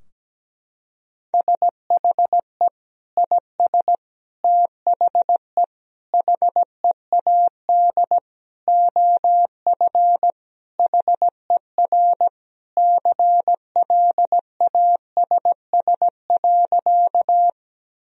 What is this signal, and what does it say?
Morse code, heard as a keyed tone, about 17 wpm, 700 Hz. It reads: SHE IS THE HEAD OF HER CLASS.
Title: Sentences from Top 1000 Words - Review Part 2 - 17wpm